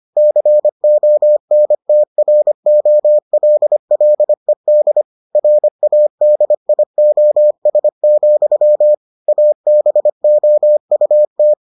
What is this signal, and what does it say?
Morse code, heard as a keyed tone, about 25 wpm, 600 Hz. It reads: CONTROLLED RADIOS, ABOUT